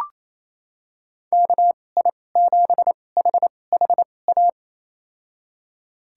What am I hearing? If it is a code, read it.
Morse code, heard as a keyed tone, about 28 wpm, spaced as for 20 wpm, 700 Hz. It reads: KI7HHA
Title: USA Call Signs - 2x Character Spacing - 28wpm